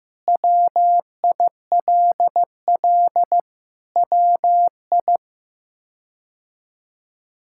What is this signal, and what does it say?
Morse code, heard as a keyed tone, about 16 wpm, 700 Hz. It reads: WILL WI
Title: K1USN SST QSO Practice - 15wpm